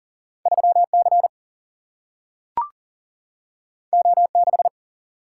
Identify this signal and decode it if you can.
Morse code, heard as a keyed tone, about 39 wpm, 700 Hz. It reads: 3C O6
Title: Single-Letter Single-Number 40wpm